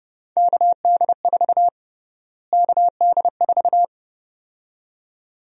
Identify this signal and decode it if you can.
Morse code, heard as a keyed tone, about 30 wpm, 700 Hz. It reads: KD4 KD4
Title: USA Call Sign Prefixes - Repeated 2x - 30wpm